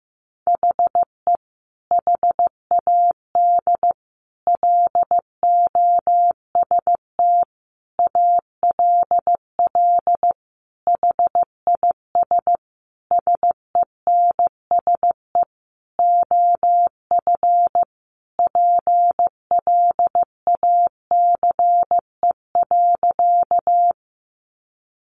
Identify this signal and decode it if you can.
Morse code, heard as a keyed tone, about 15 wpm, 700 Hz. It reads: HE HAD LOST ALL HIS SENSE OF PLACE.
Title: Sentences from Top 1400 Words 15wpm